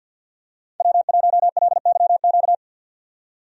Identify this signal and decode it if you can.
Morse code, heard as a keyed tone, about 50 wpm, 700 Hz. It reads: W1LYQ
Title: USA Call Signs - 5 Characters - 50wpm